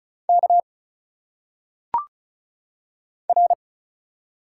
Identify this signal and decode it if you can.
Morse code, heard as a keyed tone, about 35 wpm, 700 Hz. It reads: K R
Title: Single Letters - Phonetic Pronunciation 35wpm